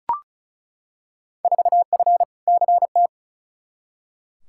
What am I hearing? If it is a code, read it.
Morse code, heard as a keyed tone, about 36 wpm, 700 Hz. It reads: 4F CT
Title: Field Day Exchanges Encore - 35wpm